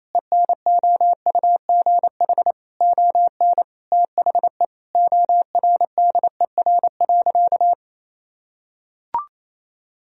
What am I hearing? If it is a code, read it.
Morse code, heard as a keyed tone, about 28 wpm, 700 Hz. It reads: ENOUGH ON THE ORDER.
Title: Sentences from Top 1000 Words - Review Part 2 - 28wpm